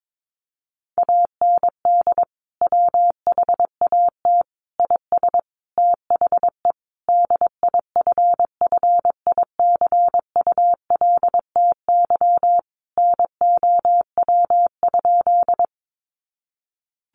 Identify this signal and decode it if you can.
Morse code, heard as a keyed tone, about 22 wpm, 700 Hz. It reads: AND WHAT IS THE DIFFICULTY NOW?
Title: Sentences from Top 1500 Words 22wpm